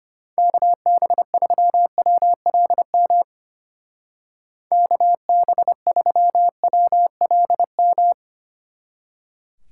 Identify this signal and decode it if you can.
Morse code, heard as a keyed tone, about 33 wpm, 700 Hz. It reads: KB3WLM KB3WLM